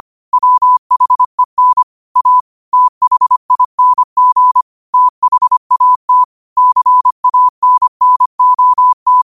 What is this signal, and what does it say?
Morse code, heard as a keyed tone, about 25 wpm, 1000 Hz. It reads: WHEN A THING THAT CANNOT
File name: qotdfiles / 25wpm